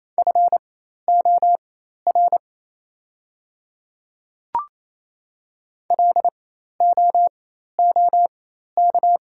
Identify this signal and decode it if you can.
Morse code, heard as a keyed tone, about 28 wpm, spaced as for 13 wpm, 690 Hz. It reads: FOR LOOK
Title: 4x Character Spacing - Top 100 Words - 28wpm